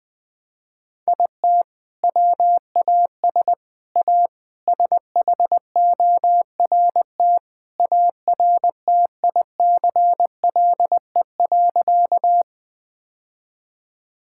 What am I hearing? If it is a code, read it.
Morse code, heard as a keyed tone, about 20 wpm, 700 Hz. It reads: IT WAS A SHORT ARTICLE.
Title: Sentences from Top 1000 Words - Review Part 5 - 20wpm